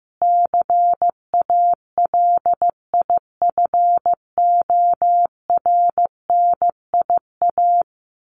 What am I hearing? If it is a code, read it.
Morse code, heard as a keyed tone, about 15 wpm, 700 Hz. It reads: CALIFORNIA
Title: Sets of 5 Words Encore2 15wpm